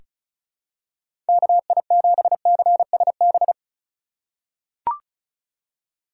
Morse code, heard as a keyed tone, about 36 wpm, 700 Hz. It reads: KI7CSB
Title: USA Call Signs - 6 Characters - 2x Character Spacing - 35wpm